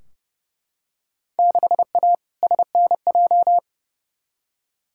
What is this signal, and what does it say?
Morse code, heard as a keyed tone, about 30 wpm, 700 Hz. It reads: 6A SNJ
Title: Field Day Exchanges Encore - 30wpm